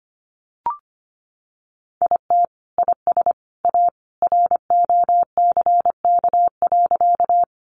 Morse code, heard as a keyed tone, about 25 wpm, 700 Hz. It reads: IT IS A ROCK.
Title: Sentences from Top 400 Words 25wpm